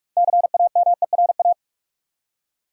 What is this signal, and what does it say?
Morse code, heard as a keyed tone, about 42 wpm, 700 Hz. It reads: CAMERA
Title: Six Letter Words - 45wpm